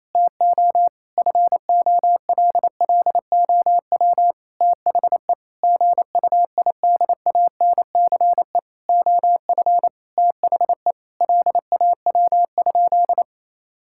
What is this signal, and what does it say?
Morse code, heard as a keyed tone, about 28 wpm, 700 Hz. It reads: TO FOLLOW THE GUIDANCE OF THE LAW?
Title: Sentences from Top 1600 Words 28wpm